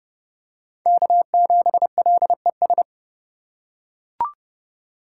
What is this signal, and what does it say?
Morse code, heard as a keyed tone, about 30 wpm, 700 Hz. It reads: K7LES
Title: USA Call Signs - 5 Characters - 4x Character Spacing - 30wpm